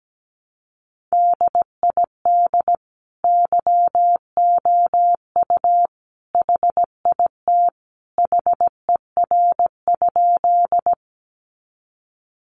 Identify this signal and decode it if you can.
Morse code, heard as a keyed tone, about 17 wpm, 700 Hz. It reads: DID YOU HIT HER?
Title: Sentences from Top 1200 Words 17wpm